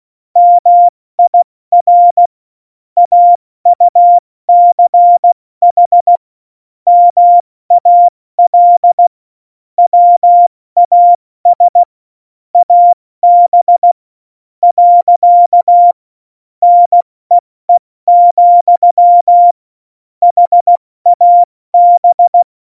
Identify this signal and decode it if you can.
Morse code, heard as a keyed tone, about 16 wpm, spaced as for 13 wpm, 700 Hz. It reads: MIR AUCH MAL WAS AB. NEE, HAB
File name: Morsekurs / Witze / Witz 08. Dezember 2020, 70 F80